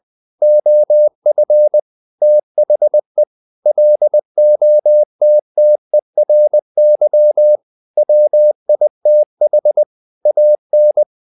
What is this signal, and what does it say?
Morse code, heard as a keyed tone, about 20 wpm, 600 Hz. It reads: OF THE LOTTERY WITH AN
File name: the old man and the sea / the old man and the sea part 1 20wpm